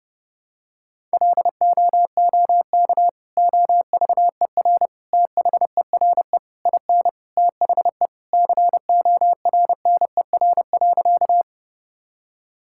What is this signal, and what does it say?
Morse code, heard as a keyed tone, about 30 wpm, 700 Hz. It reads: LOOK OVER THERE IN THE CORNER.
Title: Sentences from Top 1600 Words 30wpm